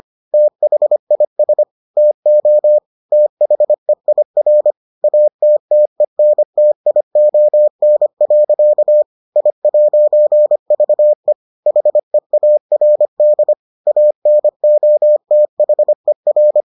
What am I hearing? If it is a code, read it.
Morse code, heard as a keyed tone, about 25 wpm, 600 Hz. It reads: THIS TO THEIR ATTENTION. I'VE HEARD ANOTHER